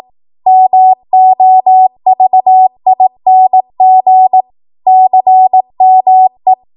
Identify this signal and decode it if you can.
Morse code, heard as a keyed tone, about 18 wpm, 750 Hz. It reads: MOVING CME
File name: Morse / 18WPM